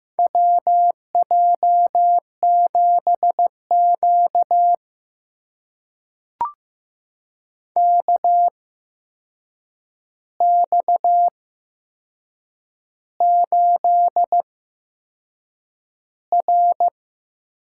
Morse code, heard as a keyed tone, about 15 wpm, spaced as for 6 wpm, 700 Hz. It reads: WJ7Q KX8R
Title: USA Call Signs - 4 Characters - 8x Character Spacing - 15wpm